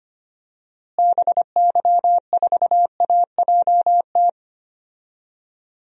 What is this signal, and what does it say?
Morse code, heard as a keyed tone, about 25 wpm, 700 Hz. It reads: BY4AJT
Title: DXCC-100 Call Signs - 6 Characters - Repeated 2x - 25wpm